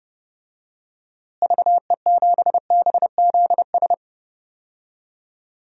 Morse code, heard as a keyed tone, about 30 wpm, 700 Hz. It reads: VE7BZS
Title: DXCC-100 Call Signs - 6 Characters - Repeated 2x - 30wpm